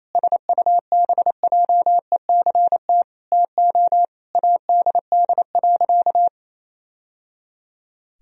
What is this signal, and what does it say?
Morse code, heard as a keyed tone, about 28 wpm, 700 Hz. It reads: SUBJECT TO ADD.
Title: Sentences from Top 1000 Words - Review Part 1 - 28wpm